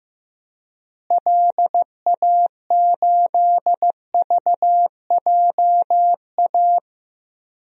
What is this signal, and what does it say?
Morse code, heard as a keyed tone, about 15 wpm, 700 Hz. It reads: LA8VJA
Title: DXCC-100 Call Signs - 6 Characters - Repeated 3x - 15wpm